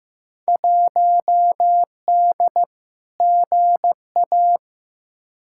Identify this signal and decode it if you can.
Morse code, heard as a keyed tone, about 15 wpm, 700 Hz. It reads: 1D GA
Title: Field Day Exchanges - 15wpm